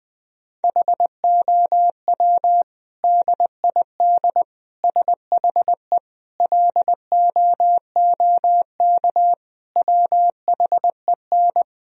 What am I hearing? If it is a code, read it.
Morse code, heard as a keyed tone, about 20 wpm, 700 Hz. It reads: HOW DID SHE LOOK WHEN
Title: Sentences from Top 1000 Words - Review Part 3 - 20wpm